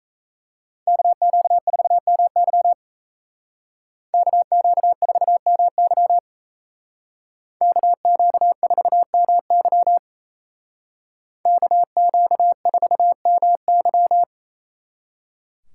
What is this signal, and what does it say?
Morse code, heard as a keyed tone, about 36 wpm, 700 Hz. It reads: KQ4MY KQ4MY KQ4MY KQ4MY